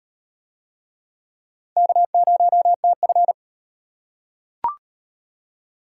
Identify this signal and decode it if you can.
Morse code, heard as a keyed tone, about 38 wpm, 700 Hz. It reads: K0TF